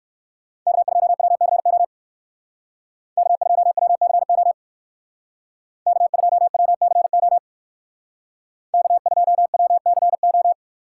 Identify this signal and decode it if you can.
Morse code, heard as a keyed tone, about 56 wpm, 700 Hz. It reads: K2WCO K2WCO K2WCO K2WCO
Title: Speed Racing - USA 1x3 Call Signs 45wpm